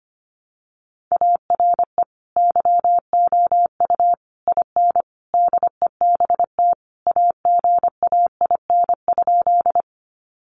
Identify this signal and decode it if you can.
Morse code, heard as a keyed tone, about 25 wpm, 700 Hz. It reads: ARE YOU IN DEBT AGAIN?
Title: Sentences from Top 1300 Words 25wpm